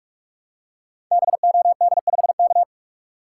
Morse code, heard as a keyed tone, about 44 wpm, 700 Hz. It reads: BOD5K
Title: Code Groups - 5 Letter:Numbers - 45wpm